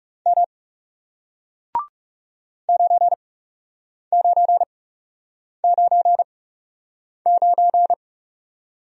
Morse code, heard as a keyed tone, about 39 wpm, spaced as for 8 wpm, 700 Hz. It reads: M 9999